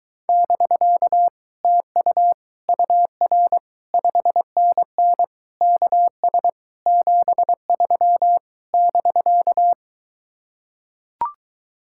Morse code, heard as a keyed tone, about 22 wpm, 700 Hz. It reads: <BK> TU UR 5NN KS 73 <BK>